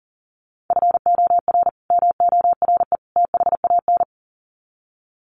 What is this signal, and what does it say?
Morse code, heard as a keyed tone, about 40 wpm, 700 Hz. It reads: FOR MORE THAN
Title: Sets of 3 Words 40wpm